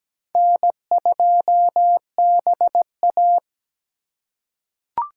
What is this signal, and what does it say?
Morse code, heard as a keyed tone, about 17 wpm, 700 Hz. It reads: N2BA
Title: USA Call Signs - 4 Characters - 4x Character Spacing - 17wpm